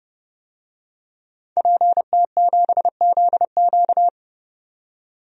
Morse code, heard as a keyed tone, about 31 wpm, 700 Hz. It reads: PT7ZQ
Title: DXCC-100 Call Signs - 5 Characters - 30wpm